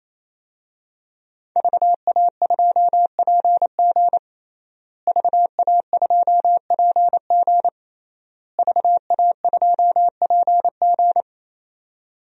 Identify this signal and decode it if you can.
Morse code, heard as a keyed tone, about 28 wpm, 700 Hz. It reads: VA2PG VA2PG VA2PG
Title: DXCC-100 Call Signs - 5 Characters - Repeated 3x - 28wpm